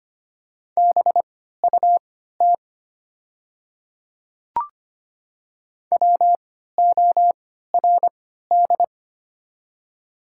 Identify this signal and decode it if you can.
Morse code, heard as a keyed tone, about 25 wpm, spaced as for 14 wpm, 700 Hz. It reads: BUT WORD
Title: 3x Character Spacing - Top 100 Words - 25wpm